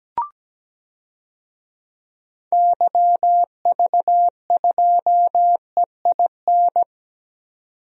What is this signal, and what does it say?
Morse code, heard as a keyed tone, about 17 wpm, 700 Hz. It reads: YV2EIN
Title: DXCC-100 Call Signs - 6 Characters - 17wpm